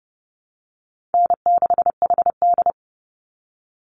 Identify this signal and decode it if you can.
Morse code, heard as a keyed tone, about 30 wpm, 700 Hz. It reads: N6HD
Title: Easy Call Signs Repeated 2x - 30wpm